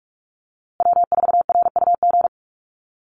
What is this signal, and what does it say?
Morse code, heard as a keyed tone, about 44 wpm, 700 Hz. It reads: W4RUG
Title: USA Call Signs Repeated 2x 45wpm